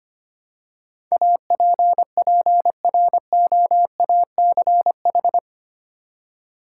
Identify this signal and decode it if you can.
Morse code, heard as a keyed tone, about 25 wpm, 700 Hz. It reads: APPROACH
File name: 8 Letter Words - First 3 Letters Given - 25wpm